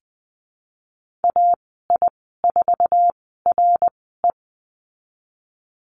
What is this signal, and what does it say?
Morse code, heard as a keyed tone, about 20 wpm, spaced as for 14 wpm, 700 Hz. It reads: AI4RE
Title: 055 Lesson 10g - Letter r - Call Signs 20wpm